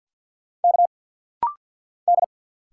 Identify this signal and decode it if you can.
Morse code, heard as a keyed tone, about 49 wpm, 700 Hz. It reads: K D
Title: Single Letter-Number - Rapid-Fire 50wpm